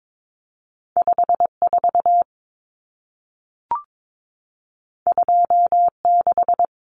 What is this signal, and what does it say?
Morse code, heard as a keyed tone, about 22 wpm, 700 Hz. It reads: 54 26
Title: Numbers - 2-Digit 22wpm